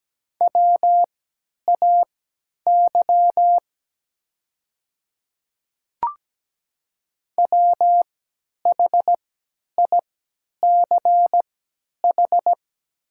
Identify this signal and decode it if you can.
Morse code, heard as a keyed tone, about 17 wpm, spaced as for 10 wpm, 700 Hz. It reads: WAY WHICH
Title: 3x Character Spacing - Top 100 Words - 17wpm